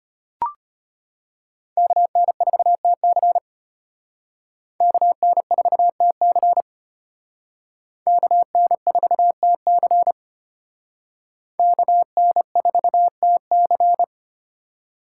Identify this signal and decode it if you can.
Morse code, heard as a keyed tone, about 32 wpm, 700 Hz. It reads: KN4TC KN4TC KN4TC KN4TC